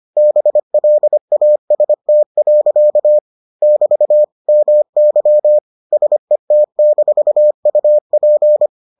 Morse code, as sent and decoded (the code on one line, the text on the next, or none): -... .-.. .- ... - .-.-.- / -...- / -- -.-- / ... - -....- ..- .--.
BLAST. <BT> MY SET-UP